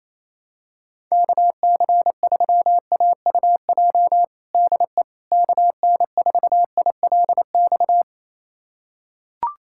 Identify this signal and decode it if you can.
Morse code, heard as a keyed tone, about 28 wpm, 700 Hz. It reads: KC3AUJ DE KN4ILX
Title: USA Callsign DE Callsign - 6 Characters - 28wpm